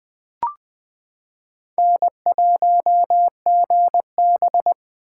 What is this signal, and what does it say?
Morse code, heard as a keyed tone, about 20 wpm, 700 Hz. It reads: N1GB
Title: 2023 SOTA Chaser Call Signs - 20wpm